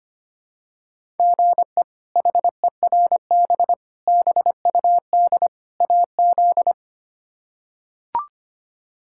Morse code, heard as a keyed tone, about 25 wpm, 700 Hz. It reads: GE HERB BUD AZ
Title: K1USN SST QSO Practice - 25wpm